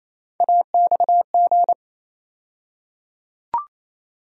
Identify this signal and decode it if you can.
Morse code, heard as a keyed tone, about 28 wpm, 700 Hz. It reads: AXG
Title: USA Call Sign Postfixes - 3 Letters 28wpm